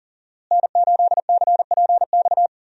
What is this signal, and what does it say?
Morse code, heard as a keyed tone, about 37 wpm, 700 Hz. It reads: N8CPX
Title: USA Call Signs - 5 Characters - Repeated 3x - 40wpm